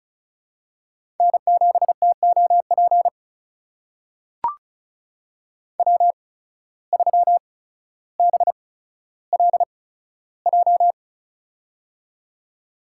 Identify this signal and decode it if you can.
Morse code, heard as a keyed tone, about 36 wpm, spaced as for 15 wpm, 700 Hz. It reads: N7TOP W3BLJ
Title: USA Call Signs - 5 Characters - 8x Character Spacing - 35wpm